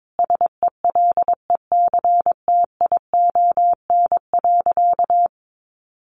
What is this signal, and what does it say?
Morse code, heard as a keyed tone, about 22 wpm, 700 Hz. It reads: SELECTION.
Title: Sentences from Top 1400 Words 22wpm